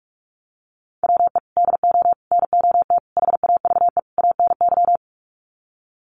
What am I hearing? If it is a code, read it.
Morse code, heard as a keyed tone, about 44 wpm, 700 Hz. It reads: WE DO NOT HAVE ANY
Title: Sets of 5 Words 45wpm